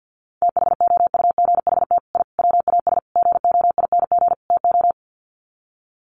Morse code, heard as a keyed tone, about 50 wpm, 700 Hz. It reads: THOUGHT I WAS GOING TO
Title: Sets of 5 Words Encore2 50wpm